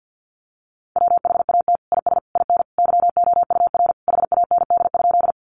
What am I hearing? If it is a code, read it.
Morse code, heard as a keyed tone, about 50 wpm, 700 Hz. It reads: WHAT IS IN YOUR HAND?
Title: Sentences from Top 200 Words 50wpm